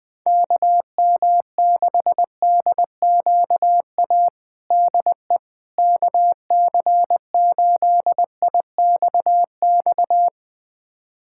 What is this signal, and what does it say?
Morse code, heard as a keyed tone, about 20 wpm, 700 Hz. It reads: KM6DQA DE KC8IXX
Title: USA Callsign DE Callsign 20wpm